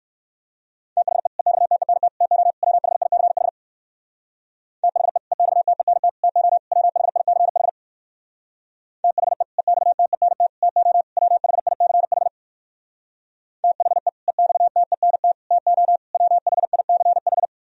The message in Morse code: THE EXTENT TO WHICH THE EXTENT TO WHICH THE EXTENT TO WHICH THE EXTENT TO WHICH

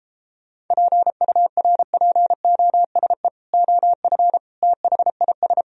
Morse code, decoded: PURPOSE OF THIS

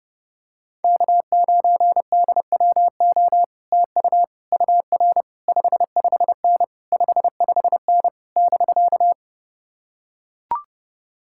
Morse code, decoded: K9DWO TU UR 55N 55N <BK>

